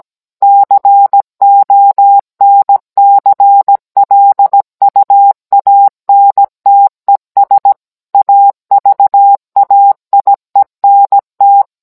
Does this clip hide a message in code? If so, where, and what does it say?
Yes: Morse code from 0.4 to 11.6 s, heard as a keyed tone, about 17 wpm, 800 Hz. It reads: CONCLUANTES AVAIENT